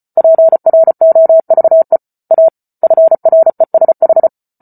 Morse code, heard as a keyed tone, about 34 wpm, 650 Hz. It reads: PROVE A FRESH